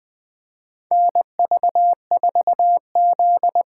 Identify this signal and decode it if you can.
Morse code, heard as a keyed tone, about 20 wpm, 700 Hz. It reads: NV4Z
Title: USA Call Signs - 4 Characters - 2x Character Spacing - 20wpm